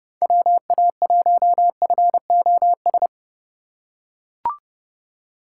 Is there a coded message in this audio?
Yes: Morse code from 0.2 to 3.1 s, heard as a keyed tone, about 29 wpm, 700 Hz. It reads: WA1FOS